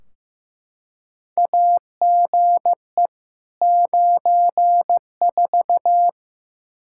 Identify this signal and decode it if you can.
Morse code, heard as a keyed tone, about 16 wpm, 700 Hz. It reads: AGE 94
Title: Practice QSO - Exchange 3 - More Info - 15wpm